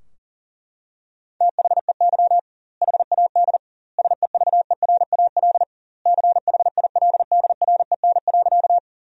Morse code, heard as a keyed tone, about 40 wpm, 700 Hz. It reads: THEY HAD SEVERAL CHILDREN.
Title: 2x Word-Spacing - Sentences from Top 400 Words - 40wpm